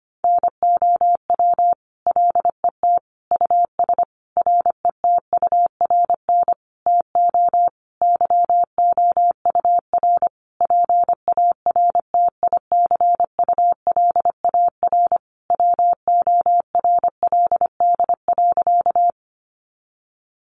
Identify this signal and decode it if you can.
Morse code, heard as a keyed tone, about 25 wpm, 700 Hz. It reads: NOW LET US RETURN TO YOUR PARTICULAR WORLD.
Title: Sentences from Top 800 Words 25wpm